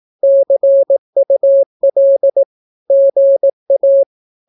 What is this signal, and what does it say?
Morse code, heard as a keyed tone, about 18 wpm, 550 Hz. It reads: CUL GA
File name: naukapodstaw / skroty / skroty